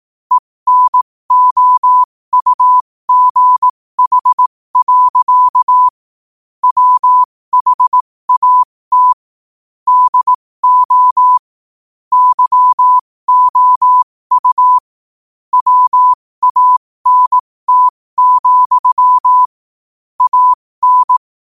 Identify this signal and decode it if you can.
Morse code, heard as a keyed tone, about 18 wpm, spaced as for 15 wpm, 1000 Hz. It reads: ENOUGH. WHAT DO YOU WANT, AN
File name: qotdfiles / 15wpm